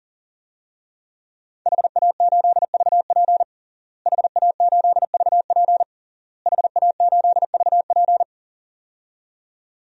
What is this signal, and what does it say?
Morse code, heard as a keyed tone, about 40 wpm, 700 Hz. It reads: HA8VP HA8VP HA8VP
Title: DXCC-100 Call Signs - 5 Characters - Repeated 3x - 40wpm